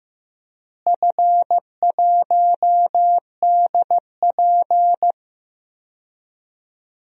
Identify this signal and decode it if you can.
Morse code, heard as a keyed tone, about 15 wpm, 700 Hz. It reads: F1DP
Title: DXCC-100 Call Signs - 4 Characters - Repeated 2x - 15wpm